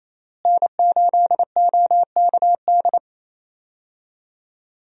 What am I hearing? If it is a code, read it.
Morse code, heard as a keyed tone, about 28 wpm, 700 Hz. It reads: N8OKD
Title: USA Call Signs - 5 Characters - Repeated 2x - 28wpm